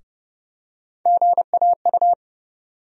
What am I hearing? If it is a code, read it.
Morse code, heard as a keyed tone, about 30 wpm, 700 Hz. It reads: GAU
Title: USA Call Sign Postfixes - 3 Letters 30wpm